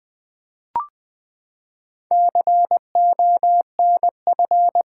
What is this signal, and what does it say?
Morse code, heard as a keyed tone, about 20 wpm, 700 Hz. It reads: CONF
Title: Build A Word - 4 Letters At A Time - 8-Letter Words - 20wpm